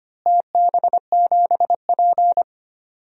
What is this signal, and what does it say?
Morse code, heard as a keyed tone, about 25 wpm, 700 Hz. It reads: TB7P
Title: DXCC-100 Call Signs - 4 Characters - Repeated 3x - 25wpm